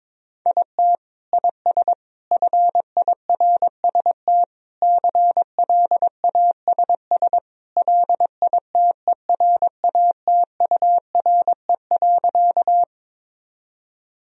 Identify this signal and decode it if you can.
Morse code, heard as a keyed tone, about 22 wpm, 700 Hz. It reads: IT IS FIRST CLASS LITERATURE.
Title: Sentences from Top 1000 Words - Review Part 3 - 22wpm